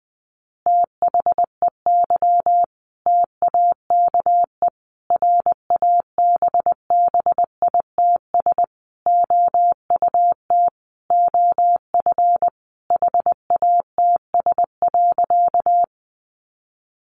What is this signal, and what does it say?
Morse code, heard as a keyed tone, about 19 wpm, 700 Hz. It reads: THEY TAKE RABBITS OUT OF HATS.
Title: Easy Phrases 20wpm